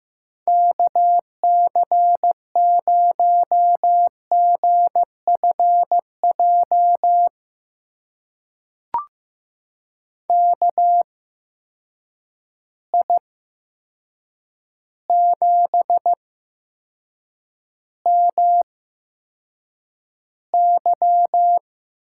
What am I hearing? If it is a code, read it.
Morse code, heard as a keyed tone, about 15 wpm, spaced as for 7 wpm, 700 Hz. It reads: KC0GFJ KI7MY
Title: USA Call Signs - 6 Characters - 8x Character Spacing - 15wpm